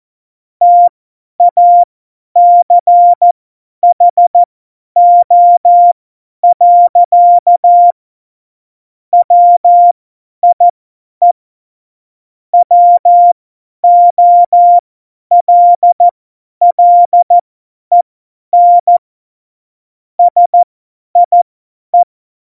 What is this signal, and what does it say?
Morse code, heard as a keyed tone, about 14 wpm, spaced as for 10 wpm, 700 Hz. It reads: TACHO. WIE WOLLEN SIE
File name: Morsekurs / Witze / Witz 14 Juli 2020, 50 F70